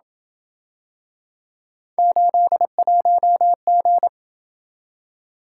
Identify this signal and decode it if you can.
Morse code, heard as a keyed tone, about 27 wpm, 700 Hz. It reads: 81G